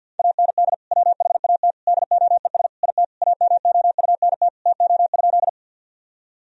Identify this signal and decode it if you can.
Morse code, heard as a keyed tone, about 50 wpm, 700 Hz. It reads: AND WHAT DOES IT AMOUNT TO?
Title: Sentences from Top 1000 Words - Review Part 4 - 50wpm